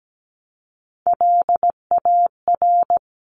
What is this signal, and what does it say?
Morse code, heard as a keyed tone, about 17 wpm, 700 Hz. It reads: LAR
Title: CWT Exchange 17wpm